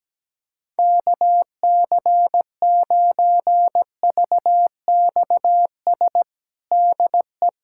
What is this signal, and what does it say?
Morse code, heard as a keyed tone, about 17 wpm, 700 Hz. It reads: KC9VXS DE
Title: USA Callsign DE Callsign - 6 Characters - 17wpm